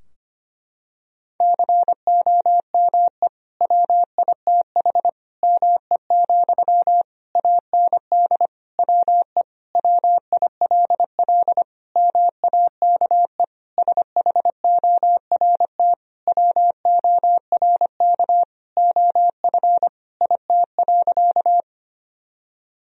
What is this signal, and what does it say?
Morse code, heard as a keyed tone, about 25 wpm, 700 Hz. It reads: COME WITH ME, AND WE WILL MAKE SHORT WORK OF IT.
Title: Sentences from Top 1000 Words - Review Part 2 - 25wpm